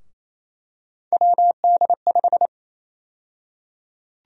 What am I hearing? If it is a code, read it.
Morse code, heard as a keyed tone, about 28 wpm, 700 Hz. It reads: WD5
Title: USA Call Sign Prefixes - 28wpm